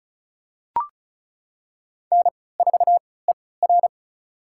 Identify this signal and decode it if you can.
Morse code, heard as a keyed tone, about 34 wpm, spaced as for 20 wpm, 700 Hz. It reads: N4ER
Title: USA Call Signs - 4 Characters - 3x Character Spacing - 35wpm